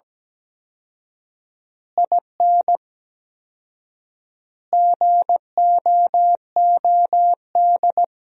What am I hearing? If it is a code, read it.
Morse code, heard as a keyed tone, about 17 wpm, 700 Hz. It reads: IN GOOD
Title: 4x Word-Spacing - Sentences from Top 200 Words - 17wpm